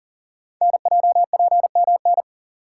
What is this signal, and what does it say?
Morse code, heard as a keyed tone, about 39 wpm, 700 Hz. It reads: NJPMN
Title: Code Groups - 5 Letters - 40wpm